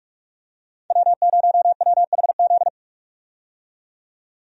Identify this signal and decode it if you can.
Morse code, heard as a keyed tone, about 43 wpm, 700 Hz. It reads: W0WHZ
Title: USA Call Signs - 5 Characters - 45wpm